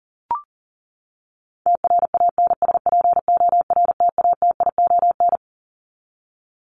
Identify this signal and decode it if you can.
Morse code, heard as a keyed tone, about 40 wpm, 700 Hz. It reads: TRANSPORTATION
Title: Top 1501-1750 Words 40wpm